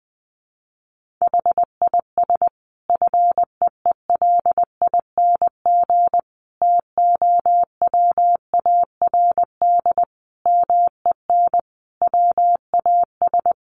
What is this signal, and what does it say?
Morse code, heard as a keyed tone, about 20 wpm, 700 Hz. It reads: HIS FEELING TOWARD MEN WAS
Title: Sentences from Top 1300 Words 20wpm